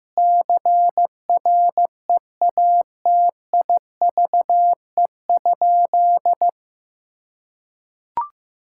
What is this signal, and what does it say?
Morse code, heard as a keyed tone, about 16 wpm, 700 Hz. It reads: CREATIVE?
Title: Sentences from Top 1000 Words - Review Part 5 - 15wpm